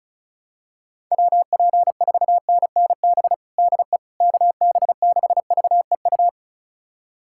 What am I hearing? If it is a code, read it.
Morse code, heard as a keyed tone, about 34 wpm, 700 Hz. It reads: WP4NNB DE KB6VEU
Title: USA Callsign DE Callsign - 6 Characters - 35wpm